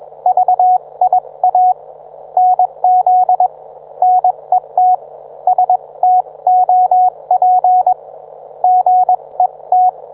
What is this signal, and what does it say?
Morse code, heard as a keyed tone, about 21 wpm, spaced as for 19 wpm, 730 Hz. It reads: VIA NZ NET STOP GET
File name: book